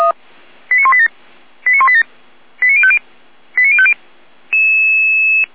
Five-tone selective call (ZVEI), heard as quick runs of five tones, tones of about 70 ms. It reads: 89187 89187 89050 89050